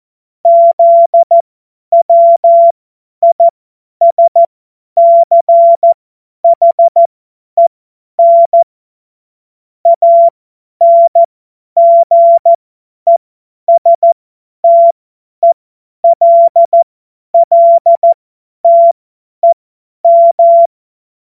Morse code, heard as a keyed tone, about 14 wpm, spaced as for 10 wpm, 680 Hz. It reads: ZWISCHEN ANGESTELLTEM
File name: Morsekurs / Witze / Witz 24. Januar 2023 50 F70